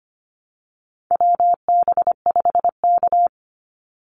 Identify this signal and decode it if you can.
Morse code, heard as a keyed tone, about 25 wpm, 700 Hz. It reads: WB5K
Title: Call Signs Repeated 2x - 25wpm